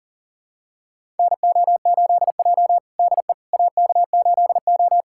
Morse code, heard as a keyed tone, about 40 wpm, 700 Hz. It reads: NO8J DE AK8O
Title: USA Callsign DE Callsign - 4 Characters - 40wpm